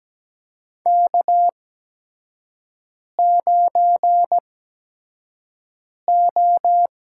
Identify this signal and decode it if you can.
Morse code, heard as a keyed tone, about 17 wpm, spaced as for 5 wpm, 700 Hz. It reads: K9O
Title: USA Call Signs - 4 Characters - 8x Character Spacing - 17wpm